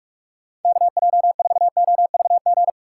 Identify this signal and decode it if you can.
Morse code, heard as a keyed tone, about 46 wpm, 700 Hz. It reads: KJ4OVG
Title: USA Call Signs - 6 Characters - 2x Character Spacing - 45wpm